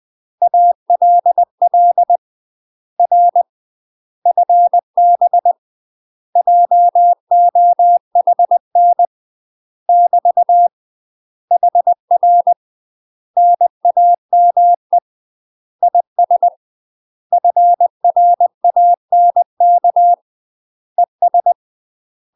Morse code, heard as a keyed tone, about 20 wpm, 700 Hz. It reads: ALL R FB JOHN <BT> HR NAME IS FRANK ES
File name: UK MASTER QSOs TRAINING / 20WPM / UK MASTER QSOs 0074